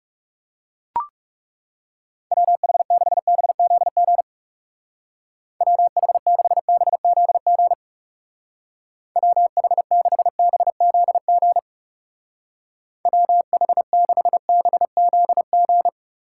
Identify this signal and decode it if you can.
Morse code, heard as a keyed tone, about 36 wpm, 700 Hz. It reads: WH6BZG WH6BZG WH6BZG WH6BZG
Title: Speed Racing - USA Call Signs 30wpm